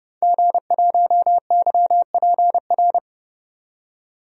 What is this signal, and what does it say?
Morse code, heard as a keyed tone, about 30 wpm, 700 Hz. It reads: G1YPR
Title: DXCC-100 Call Signs - 5 Characters - 30wpm